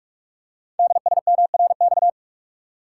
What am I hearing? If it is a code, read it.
Morse code, heard as a keyed tone, about 44 wpm, 700 Hz. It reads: DSMRX
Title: Code Groups - 5 Letters - 45wpm